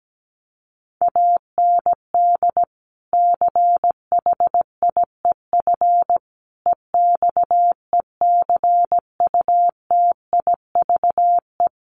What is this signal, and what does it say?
Morse code, heard as a keyed tone, about 17 wpm, 700 Hz. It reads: AND CHIEF EXECUTIVE
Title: Sets of 5 Words Encore2 17wpm